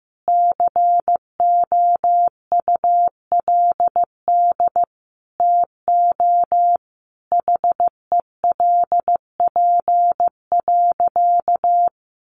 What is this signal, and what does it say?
Morse code, heard as a keyed tone, about 15 wpm, 700 Hz. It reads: COULD TO HELP.
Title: Sentences from Top 1100 Words 15wpm